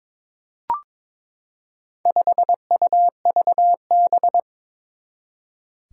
Morse code, heard as a keyed tone, about 22 wpm, 700 Hz. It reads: 5UVB
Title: Four Challenging Letter-Number Combinations 22wpm